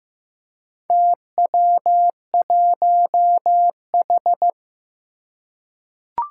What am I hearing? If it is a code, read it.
Morse code, heard as a keyed tone, about 15 wpm, 700 Hz. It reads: TW1H
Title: Four Challenging Letter-Number Combinations 15wpm